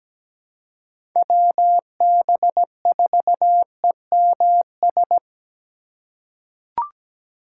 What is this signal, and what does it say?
Morse code, heard as a keyed tone, about 17 wpm, 700 Hz. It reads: WB4EMS
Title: USA Call Signs - 6 Characters - 2x Character Spacing - 17wpm